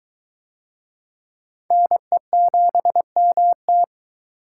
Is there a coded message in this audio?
Yes: Morse code from 1.7 to 3.8 s, heard as a keyed tone, about 23 wpm, 700 Hz. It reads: NE7MT